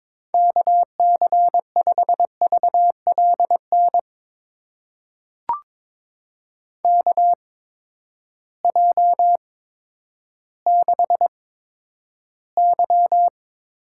Morse code, heard as a keyed tone, about 21 wpm, spaced as for 11 wpm, 700 Hz. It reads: KC5VLN KJ6Y
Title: USA Call Signs - 6 Characters - 8x Character Spacing - 22wpm